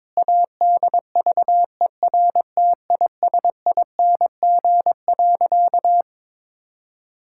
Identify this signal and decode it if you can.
Morse code, heard as a keyed tone, about 22 wpm, 700 Hz. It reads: ADVERTISING.
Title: Sentences from Top 1000 Words - Review Part 6 - 22wpm